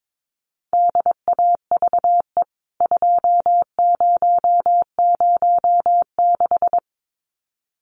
Morse code, heard as a keyed tone, about 22 wpm, 700 Hz. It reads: DAVE 2006
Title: CWT Exchange 22wpm